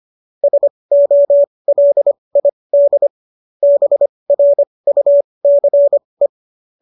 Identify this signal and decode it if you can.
Morse code, heard as a keyed tone, about 25 wpm, spaced as for 20 wpm, 590 Hz. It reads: SOLID BRUCE
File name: qso201 20